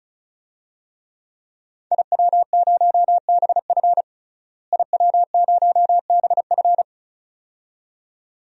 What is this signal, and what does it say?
Morse code, heard as a keyed tone, about 36 wpm, 700 Hz. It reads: IW0BF IW0BF
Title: DXCC-100 Call Signs - 5 Characters - Repeated 2x - 35wpm